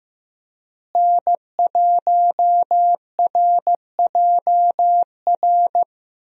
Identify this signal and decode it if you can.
Morse code, heard as a keyed tone, about 16 wpm, 700 Hz. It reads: N1RJR
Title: USA Call Signs - 5 Characters - Repeated 3x - 15wpm